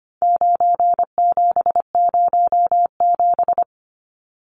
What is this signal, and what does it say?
Morse code, heard as a keyed tone, about 25 wpm, 700 Hz. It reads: 9707
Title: Numbers - 4-Digit 25wpm